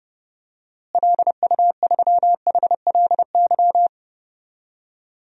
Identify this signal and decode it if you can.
Morse code, heard as a keyed tone, about 32 wpm, 700 Hz. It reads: LU3HLY
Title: DXCC-100 Call Signs - 6 Characters - Repeated 3x - 30wpm